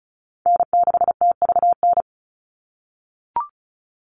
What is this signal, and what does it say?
Morse code, heard as a keyed tone, about 36 wpm, 700 Hz. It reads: N6TVN
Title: Call Signs - 35wpm